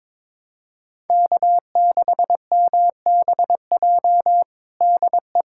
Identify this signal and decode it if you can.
Morse code, heard as a keyed tone, about 22 wpm, 700 Hz. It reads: K6MBJ DE
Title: USA Callsign DE Callsign 22wpm